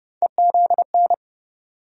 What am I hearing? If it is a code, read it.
Morse code, heard as a keyed tone, about 30 wpm, 700 Hz. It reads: EZN
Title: USA Call Sign Postfixes - 3 Letters 30wpm